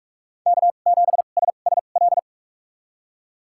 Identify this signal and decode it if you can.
Morse code, heard as a keyed tone, about 46 wpm, spaced as for 32 wpm, 700 Hz. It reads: K7SSL
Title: USA Call Signs - 5 Characters - 2x Character Spacing - 45wpm